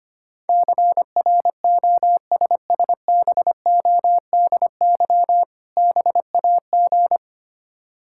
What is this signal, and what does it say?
Morse code, heard as a keyed tone, about 25 wpm, 700 Hz. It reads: CROSSBODY BAG
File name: practice / 25wpm / Difficult Words - Speed-Racing with Hints - 25wpm